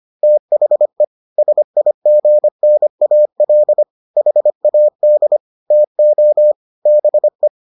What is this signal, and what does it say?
Morse code, heard as a keyed tone, about 25 wpm, 600 Hz. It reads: THE SIGNAL HAD TO BE